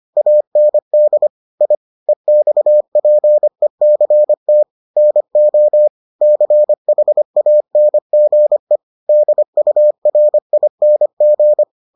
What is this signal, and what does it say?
Morse code, heard as a keyed tone, about 25 wpm, 600 Hz. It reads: AND I EXPECT NO CHANGE DURING